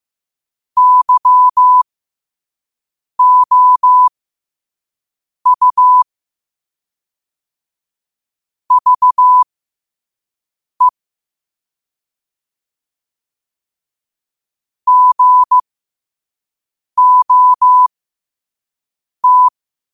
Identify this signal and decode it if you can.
Morse code, heard as a keyed tone, about 15 wpm, spaced as for 5 wpm, 1000 Hz. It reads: YOU VE GOT
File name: qotdfiles / 5wpm